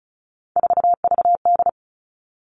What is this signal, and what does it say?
Morse code, heard as a keyed tone, about 34 wpm, 700 Hz. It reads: 4VD
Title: All Cut Numbers - 3-Digit 35wpm